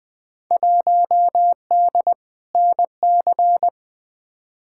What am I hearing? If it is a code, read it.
Morse code, heard as a keyed tone, about 20 wpm, 700 Hz. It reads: 1D NC
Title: Field Day Exchanges Encore - 20wpm